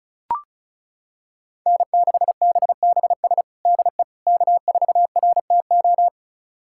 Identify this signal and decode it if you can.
Morse code, heard as a keyed tone, about 36 wpm, 700 Hz. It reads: N6BBS DE K4RTO
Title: USA Callsign DE Callsign - 5 Characters - 35wpm